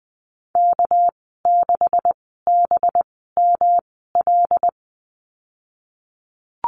Morse code, heard as a keyed tone, about 20 wpm, spaced as for 15 wpm, 700 Hz. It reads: K6BML
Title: 200 Lesson 32g - Letter k - Call Signs 20wpm